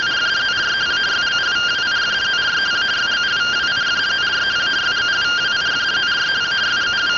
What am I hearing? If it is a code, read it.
Two-tone teleprinter signal, about 170 Hz shift.